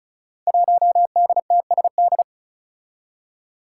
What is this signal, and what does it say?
Morse code, heard as a keyed tone, about 36 wpm, 700 Hz. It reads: 1DTSD
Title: Five Challenging Letter-Number Combinations 35wpm